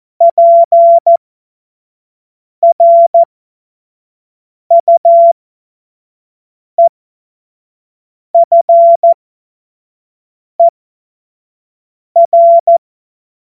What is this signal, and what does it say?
Morse code, heard as a keyed tone, about 14 wpm, spaced as for 5 wpm, 680 Hz. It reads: PRUEFER
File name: Morsekurs / Witze / Witz 16. Mai 2023 25 F70